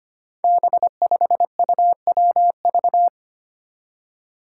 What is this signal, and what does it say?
Morse code, heard as a keyed tone, about 25 wpm, 700 Hz. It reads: B5UWV
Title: Five Challenging Letter-Number Combinations 25wpm